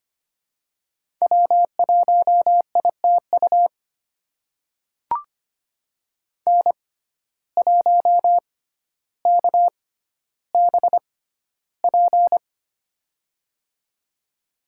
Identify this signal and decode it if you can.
Morse code, heard as a keyed tone, about 25 wpm, spaced as for 13 wpm, 700 Hz. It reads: W1ITU N1KBP
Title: USA Call Signs - 5 Characters - 6x Character Spacing - 25wpm